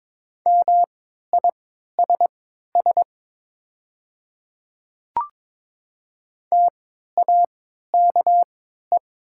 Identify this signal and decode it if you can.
Morse code, heard as a keyed tone, about 22 wpm, spaced as for 12 wpm, 700 Hz. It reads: MISS TAKE
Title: 3x Character Spacing - Top 500 Words - 22wpm